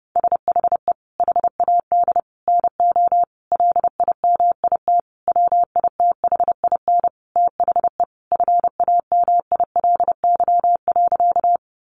Morse code, heard as a keyed tone, about 30 wpm, 700 Hz. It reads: SHE HAD NO LIMIT WITHIN THE FAMILY.
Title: Sentences from Top 1500 Words 30wpm